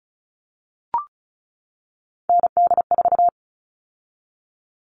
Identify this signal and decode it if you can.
Morse code, heard as a keyed tone, about 34 wpm, 700 Hz. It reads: ND4
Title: All Cut Numbers - 3-Digit 35wpm